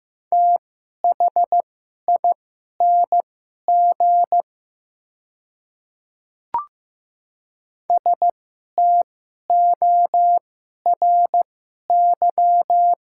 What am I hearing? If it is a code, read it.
Morse code, heard as a keyed tone, about 15 wpm, spaced as for 11 wpm, 700 Hz. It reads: THING STORY
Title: 2x Character Spacing - Top 500 Words - 15wpm